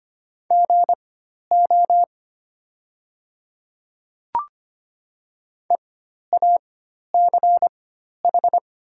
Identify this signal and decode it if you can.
Morse code, heard as a keyed tone, about 25 wpm, spaced as for 12 wpm, 700 Hz. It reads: GO EACH
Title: 4x Character Spacing - Top 100 Words - 25wpm